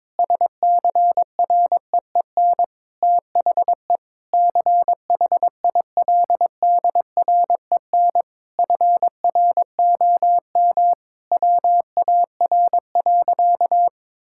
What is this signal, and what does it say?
Morse code, heard as a keyed tone, about 21 wpm, 700 Hz. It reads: SCREEN THE CHILDREN FROM WAR.
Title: Sentences from Top 1000 Words - Review Part 6 - 22wpm